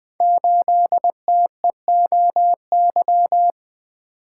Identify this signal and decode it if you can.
Morse code, heard as a keyed tone, about 20 wpm, 700 Hz. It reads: 8TEOY
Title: Code Groups - 5 Letter:Numbers - 20wpm